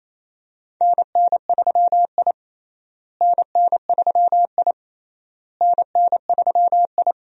Morse code, heard as a keyed tone, about 28 wpm, 700 Hz. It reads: NN3I NN3I NN3I
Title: USA Call Signs - 4 Characters - Repeated 3x - 28wpm